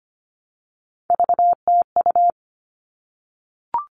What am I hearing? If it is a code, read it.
Morse code, heard as a keyed tone, about 25 wpm, 700 Hz. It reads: VTU
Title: All Cut Numbers - 3-Digit 25wpm